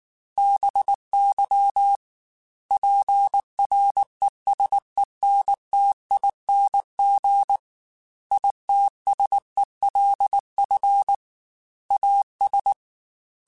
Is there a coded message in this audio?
Yes: Morse code from 0.4 to 12.7 s, heard as a keyed tone, about 19 wpm, 800 Hz. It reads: BY PRESENTING ITSELF AS